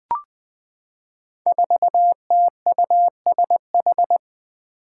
Morse code, heard as a keyed tone, about 20 wpm, 700 Hz. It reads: 4TUSH
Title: Five Challenging Letter-Number Combinations 20wpm